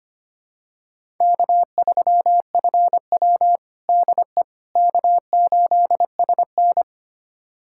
Morse code, heard as a keyed tone, about 25 wpm, 700 Hz. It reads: K3FW DE K8SN
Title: USA Callsign DE Callsign - 4 Characters - 25wpm